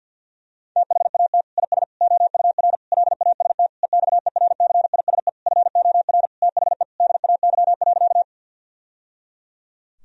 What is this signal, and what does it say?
Morse code, heard as a keyed tone, about 52 wpm, 700 Hz. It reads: THAT IS OUR LAST EXERCISE FOR THE DAY.
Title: Sentences from Top 1000 Words - Review Part 4 - 50wpm